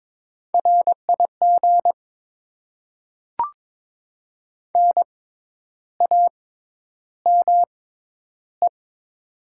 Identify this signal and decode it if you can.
Morse code, heard as a keyed tone, about 22 wpm, spaced as for 10 wpm, 700 Hz. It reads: RIG NAME
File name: Top 20 QSO Elements - 6x Character Spacing - 22wpm